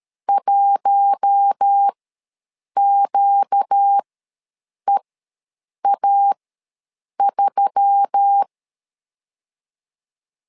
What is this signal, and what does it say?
Morse code, heard as a keyed tone, about 13 wpm, spaced as for 7 wpm, 800 Hz. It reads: 1QEA3
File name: LEZIONI / 4 Q1S3 ws 12wpm 2038